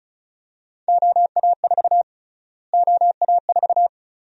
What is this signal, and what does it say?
Morse code, heard as a keyed tone, about 34 wpm, 700 Hz. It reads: OA4 OA4
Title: USA Call Sign Prefixes - Repeated 2x - 35wpm